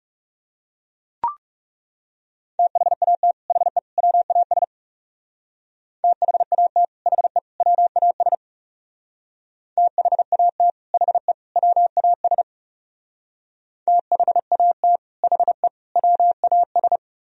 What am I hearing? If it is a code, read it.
Morse code, heard as a keyed tone, about 35 wpm, 700 Hz. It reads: THAT HE WAS THAT HE WAS THAT HE WAS THAT HE WAS
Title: Speed-Racing - Sets of 3 Words 30wpm